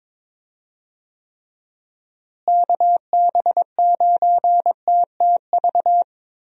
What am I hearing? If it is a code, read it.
Morse code, heard as a keyed tone, about 22 wpm, 700 Hz. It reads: KB9TTV
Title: USA Call Signs - 6 Characters - Repeated 3x - 22wpm